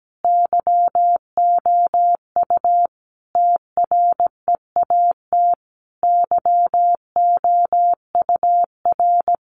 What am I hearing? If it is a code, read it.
Morse code, heard as a keyed tone, about 17 wpm, 700 Hz. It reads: YOU TREAT YOUR
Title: Sentences from Top 1300 Words 17wpm